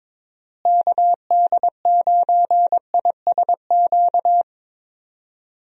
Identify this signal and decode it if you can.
Morse code, heard as a keyed tone, about 22 wpm, 700 Hz. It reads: KD9ISQ
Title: USA Call Signs - 6 Characters - Repeated 3x - 22wpm